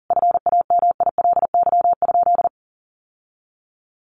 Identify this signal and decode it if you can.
Morse code, heard as a keyed tone, about 40 wpm, 700 Hz. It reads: FAMILY?
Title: Sentences from Top 1400 Words 40wpm